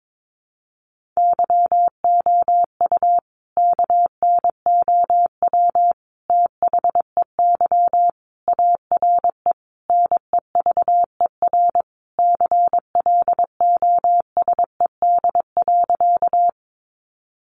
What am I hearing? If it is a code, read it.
Morse code, heard as a keyed tone, about 22 wpm, 700 Hz. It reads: YOU KNOW THEY ARE NEVER CLOSED.
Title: Sentences from Top 1400 Words 22wpm